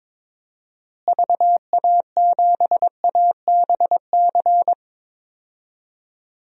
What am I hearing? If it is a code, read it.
Morse code, heard as a keyed tone, about 23 wpm, 700 Hz. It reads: VA7ABC
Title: DXCC-100 Call Signs - 6 Characters - Repeated 2x - 22wpm